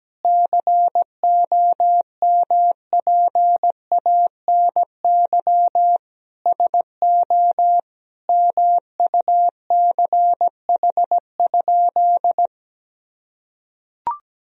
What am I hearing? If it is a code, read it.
Morse code, heard as a keyed tone, about 17 wpm, 700 Hz. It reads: COMPANY SO MUCH?
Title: Sentences from Top 1000 Words - Review Part 3 - 17wpm